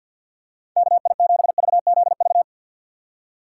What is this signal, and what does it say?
Morse code, heard as a keyed tone, about 51 wpm, 700 Hz. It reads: KI7VZV
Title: USA Call Signs - 6 Characters - 3x Character Spacing - 50wpm